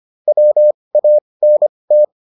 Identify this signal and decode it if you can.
Morse code, heard as a keyed tone, about 25 wpm, spaced as for 20 wpm, 600 Hz. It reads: WANT